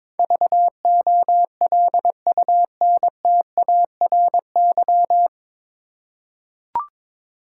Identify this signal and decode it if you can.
Morse code, heard as a keyed tone, about 22 wpm, 700 Hz. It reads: VOLUNTARY
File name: 9 Letter Words - First 3 Letters Given - 22wpm